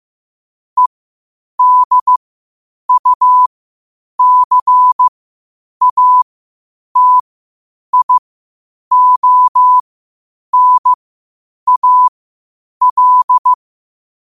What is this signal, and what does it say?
Morse code, heard as a keyed tone, about 15 wpm, spaced as for 8 wpm, 1000 Hz. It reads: EDUCATIONAL
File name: qotdfiles / 8wpm